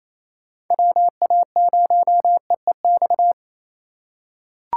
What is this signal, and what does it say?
Morse code, heard as a keyed tone, about 27 wpm, 700 Hz. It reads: WA0EEX
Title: USA Call Signs - 6 Characters - 4x Character Spacing - 28wpm